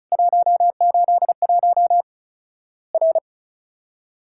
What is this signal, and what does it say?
Morse code, heard as a keyed tone, about 36 wpm, 700 Hz. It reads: 181 R
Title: Worked All Europe DX Contest - QTC Exchange - 35wpm